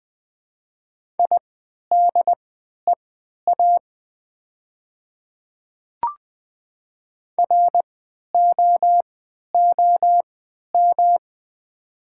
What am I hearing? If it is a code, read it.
Morse code, heard as a keyed tone, about 20 wpm, spaced as for 11 wpm, 700 Hz. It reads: IDEA ROOM
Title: 3x Character Spacing - Top 300 Words - 20wpm